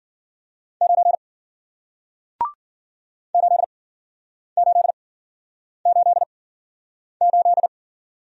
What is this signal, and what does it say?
Morse code, heard as a keyed tone, about 56 wpm, spaced as for 9 wpm, 700 Hz. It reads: 9 8888